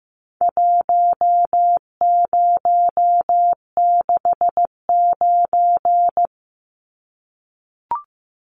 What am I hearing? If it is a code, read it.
Morse code, heard as a keyed tone, about 15 wpm, 700 Hz. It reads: 1069
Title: Numbers - Contest Style - 4-Digit 15wpm